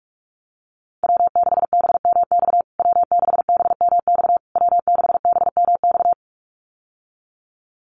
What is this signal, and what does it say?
Morse code, heard as a keyed tone, about 46 wpm, 700 Hz. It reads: W6BMX W6BMX W6BMX
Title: USA Call Signs Repeated 3x 45wpm